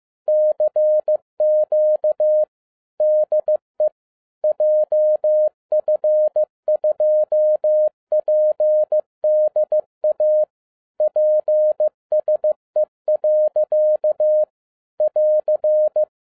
Morse code, heard as a keyed tone, about 15 wpm, 610 Hz. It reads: CQ DE JF2PDA PSE. <AR>
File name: 2011-07-mp3 / n02 JF2PDA 15wpm